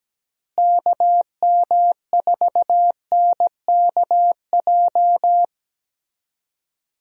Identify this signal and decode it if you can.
Morse code, heard as a keyed tone, about 17 wpm, 700 Hz. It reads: KM4NKJ
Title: USA Call Signs - 6 Characters - Repeated 2x - 17wpm